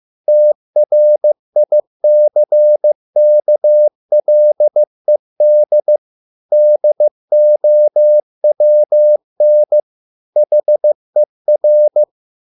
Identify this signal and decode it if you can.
Morse code, heard as a keyed tone, about 15 wpm, 600 Hz. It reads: TRICKLED DOWN HER